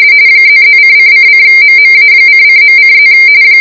RTTY, about 45.45 baud. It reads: ++::::MMA::.:+